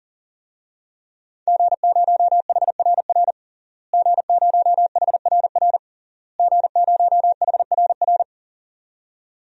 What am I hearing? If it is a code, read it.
Morse code, heard as a keyed tone, about 40 wpm, 700 Hz. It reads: G0HRR G0HRR G0HRR
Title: DXCC-100 Call Signs - 5 Characters - Repeated 3x - 40wpm